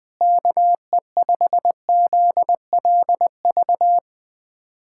Morse code, heard as a keyed tone, about 20 wpm, 700 Hz. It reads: KE5ZLV
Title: USA Callsign DE Callsign 20wpm